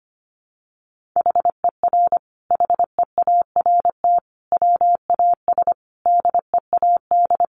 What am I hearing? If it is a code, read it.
Morse code, heard as a keyed tone, about 25 wpm, 700 Hz. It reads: HER HEART WAS DEAD
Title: Sentences from Top 1500 Words 25wpm